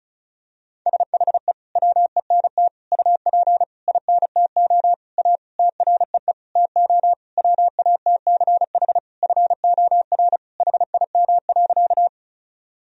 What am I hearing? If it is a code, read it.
Morse code, heard as a keyed tone, about 36 wpm, 700 Hz. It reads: SHE WENT UP INTO A TREE TO WATCH FOR HIM.
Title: Sentences from Top 1000 Words - Review Part 1 - 35wpm